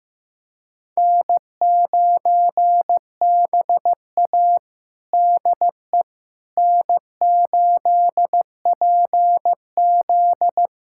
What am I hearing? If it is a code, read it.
Morse code, heard as a keyed tone, about 16 wpm, 700 Hz. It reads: N9BA DE N8PZ
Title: USA Callsign DE Callsign - 4 Characters - 15wpm